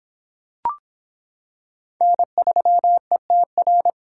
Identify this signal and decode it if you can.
Morse code, heard as a keyed tone, about 25 wpm, 700 Hz. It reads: N3ETR